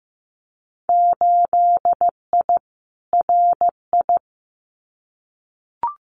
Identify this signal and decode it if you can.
Morse code, heard as a keyed tone, about 15 wpm, 700 Hz. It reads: 8I RI
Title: Winter Field Day Exchanges Encore 15wpm